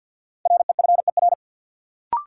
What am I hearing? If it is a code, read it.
Morse code, heard as a keyed tone, about 48 wpm, 700 Hz. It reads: REFER
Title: Sentences from Top 1000 Words - New Word Drill - 50wpm